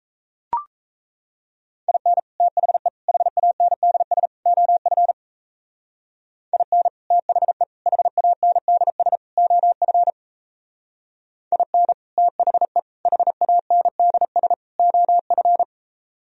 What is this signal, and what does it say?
Morse code, heard as a keyed tone, about 35 wpm, 700 Hz. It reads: IN THE HANDS OF IN THE HANDS OF IN THE HANDS OF
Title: Speed-Racing - Sets of 4 Words 28wpm